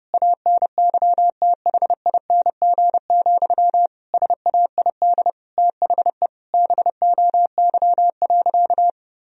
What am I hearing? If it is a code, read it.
Morse code, heard as a keyed tone, about 30 wpm, 700 Hz. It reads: ANYTHING, SAID THE BOY.
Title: Sentences from Top 1000 Words - Review Part 4 - 30wpm